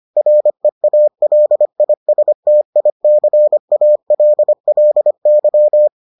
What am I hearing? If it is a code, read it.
Morse code, heard as a keyed tone, about 25 wpm, 600 Hz. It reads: REALISTICALLY